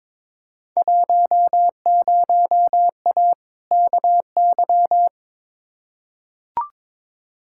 Morse code, heard as a keyed tone, about 22 wpm, 700 Hz. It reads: 10A KY
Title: Field Day Exchanges Encore - 22wpm